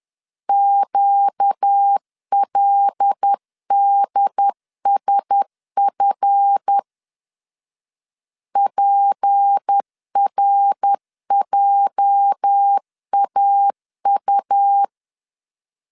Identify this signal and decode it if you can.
Morse code, heard as a keyed tone, about 10 wpm, 800 Hz. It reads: QLDSF PRJAU
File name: LEZIONI / 8 YJ45 / 8 YJ 10wpm 2651